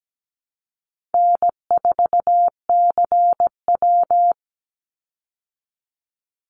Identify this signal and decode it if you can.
Morse code, heard as a keyed tone, about 17 wpm, 700 Hz. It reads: N4CW
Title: Easy Call Signs Repeated 3x - 17wpm